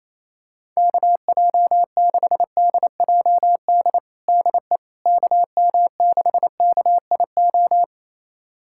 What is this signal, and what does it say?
Morse code, heard as a keyed tone, about 27 wpm, 700 Hz. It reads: KJ6DJD DE KM6KIO
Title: USA Callsign DE Callsign 28wpm